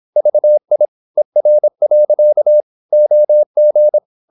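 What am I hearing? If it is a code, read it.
Morse code, heard as a keyed tone, about 26 wpm, 600 Hz. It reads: VI ER. OG
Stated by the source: High-speed (QRQ) Morse at 26 wpm.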